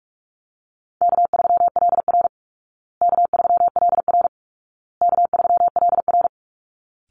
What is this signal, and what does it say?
Morse code, heard as a keyed tone, about 45 wpm, 700 Hz. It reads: K3LR K3LR K3LR
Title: Call Signs Repeated 2x - 45wpm